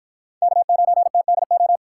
Morse code, heard as a keyed tone, about 53 wpm, 700 Hz. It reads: K9TDO